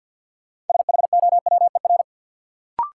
Morse code, heard as a keyed tone, about 53 wpm, 700 Hz. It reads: SHOWER
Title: Sentences from Top 1600 Words - New Word Drill Alt - 50wpm